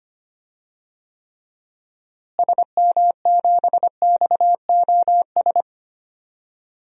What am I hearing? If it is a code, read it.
Morse code, heard as a keyed tone, about 25 wpm, 700 Hz. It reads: SM7XOS
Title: DXCC-100 Call Signs - 6 Characters - Repeated 2x - 25wpm